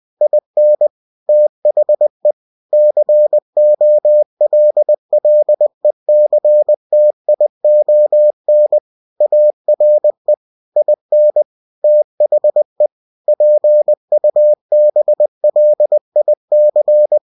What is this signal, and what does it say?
Morse code, heard as a keyed tone, about 20 wpm, 600 Hz. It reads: IN THE COLLECTION ARE IN THE PUBLIC
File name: cw-books / winnie-the-pooh / winnie-20wpm-page-0114